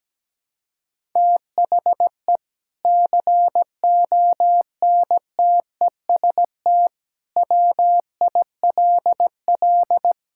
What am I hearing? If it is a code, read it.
Morse code, heard as a keyed tone, about 17 wpm, 700 Hz. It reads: THE CONTEST WILL
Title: Sentences from Top 1600 Words 17wpm